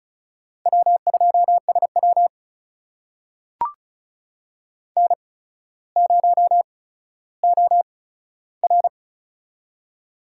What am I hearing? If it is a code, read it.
Morse code, heard as a keyed tone, about 36 wpm, spaced as for 15 wpm, 700 Hz. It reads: W2SW N0OR
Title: USA Call Signs - 4 Characters - 8x Character Spacing - 35wpm